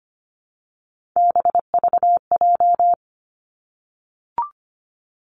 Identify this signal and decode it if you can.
Morse code, heard as a keyed tone, about 25 wpm, 700 Hz. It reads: BVJ
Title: Three Challenging Letter-Number Combinations 25wpm